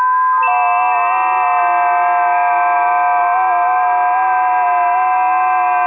FT8.
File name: tests / 20m busy / test 09